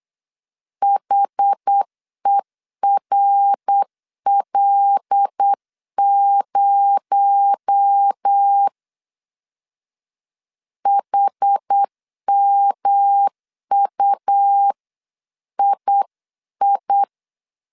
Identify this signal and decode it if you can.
Morse code, heard as a keyed tone, about 8 wpm, 800 Hz. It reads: HERL0 HMUII
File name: 8wpm